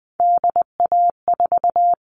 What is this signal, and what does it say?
Morse code, heard as a keyed tone, about 20 wpm, 700 Hz. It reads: DA4
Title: All Cut Numbers - 3-Digit 20wpm